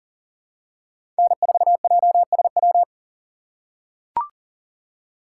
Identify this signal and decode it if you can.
Morse code, heard as a keyed tone, about 40 wpm, 700 Hz. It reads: N4JSW
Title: USA Call Signs - 5 Characters - 4x Character Spacing - 40wpm